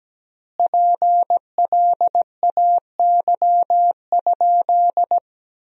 Morse code, heard as a keyed tone, about 17 wpm, 700 Hz. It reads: PLAY?
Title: 3x Word-Spacing - Sentences from Top 400 Words - 17wpm